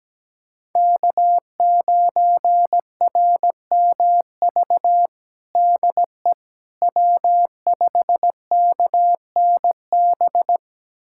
Morse code, heard as a keyed tone, about 17 wpm, 700 Hz. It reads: K9RMV DE W5KNB
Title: USA Callsign DE Callsign - 5 Characters - 17wpm